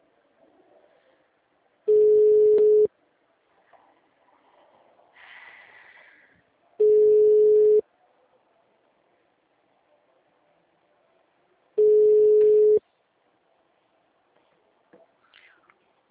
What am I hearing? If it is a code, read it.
Ringback tone (Europe, 425 Hz).
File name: out- -2010- - - .22